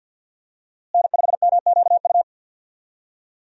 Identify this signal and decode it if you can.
Morse code, heard as a keyed tone, about 52 wpm, 700 Hz. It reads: N5MQU
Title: USA Call Signs - 5 Characters - Repeated 2x - 50wpm